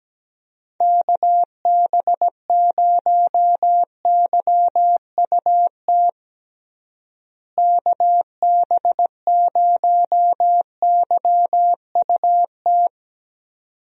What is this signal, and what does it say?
Morse code, heard as a keyed tone, about 17 wpm, 700 Hz. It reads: KB0YUT KB0YUT
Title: USA Call Signs - 6 Characters - Repeated 2x - 17wpm